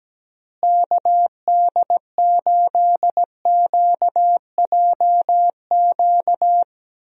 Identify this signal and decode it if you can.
Morse code, heard as a keyed tone, about 17 wpm, 700 Hz. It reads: KD8QJQ
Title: USA Call Signs - 6 Characters - Repeated 2x - 17wpm